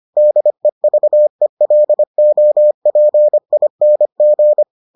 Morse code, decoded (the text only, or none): DEVELOPING